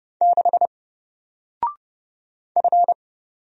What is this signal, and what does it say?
Morse code, heard as a keyed tone, about 30 wpm, 700 Hz. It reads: 6 F
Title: Single Letter-Number Alt 30wpm